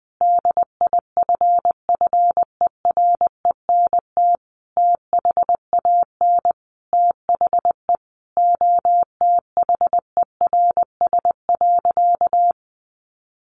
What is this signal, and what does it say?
Morse code, heard as a keyed tone, about 20 wpm, 700 Hz. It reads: DIFFERENT THAN THE OTHERS.
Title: Sentences from Top 1500 Words 20wpm